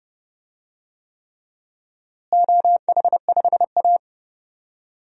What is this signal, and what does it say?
Morse code, heard as a keyed tone, about 30 wpm, 700 Hz. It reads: OH5A
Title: DXCC-100 Call Signs - 4 Characters - Repeated 3x - 30wpm